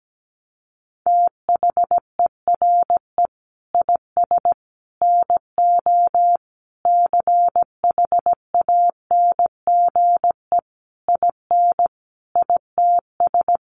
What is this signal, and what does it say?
Morse code, heard as a keyed tone, about 17 wpm, 700 Hz. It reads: THERE IS NO CHANGE IN ITS
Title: Sentences from Top 1300 Words 17wpm